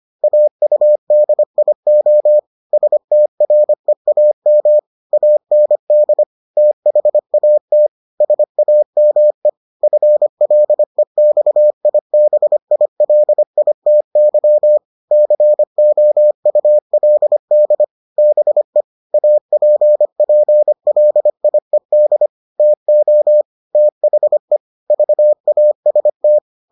Morse code, heard as a keyed tone, about 25 wpm, 600 Hz. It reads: AUDIO STREAM AND THAT SAME FLEXIBILITY COULD BE APPLIED TO THE VAST